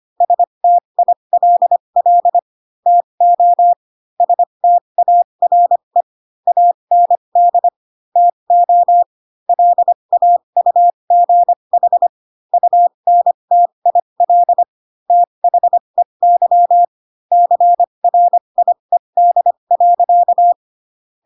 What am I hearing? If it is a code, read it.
Morse code, heard as a keyed tone, about 25 wpm, spaced as for 22 wpm, 700 Hz. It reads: STILL TO STARE AND TO LAUGH UNTIL THEY CRIED.
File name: Pinocchio / Pinocchio ch3 18 25 ik1hge